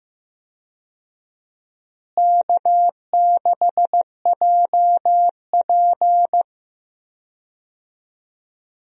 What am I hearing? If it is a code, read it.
Morse code, heard as a keyed tone, about 15 wpm, 700 Hz. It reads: K6JP
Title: USA Call Signs - 4 Characters - Repeated 3x - 15wpm